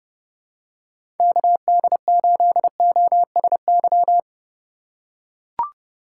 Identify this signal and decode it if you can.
Morse code, heard as a keyed tone, about 30 wpm, 700 Hz. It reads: KD8OSY
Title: USA Call Signs - 6 Characters - 4x Character Spacing - 30wpm